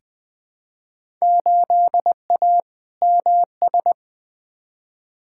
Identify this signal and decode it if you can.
Morse code, heard as a keyed tone, about 20 wpm, 700 Hz. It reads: 8A MS
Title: Field Day Exchanges - 20wpm